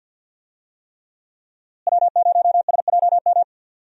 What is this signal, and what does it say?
Morse code, heard as a keyed tone, about 51 wpm, 700 Hz. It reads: W0SJM
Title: USA Call Signs - 5 Characters - Repeated 3x - 50wpm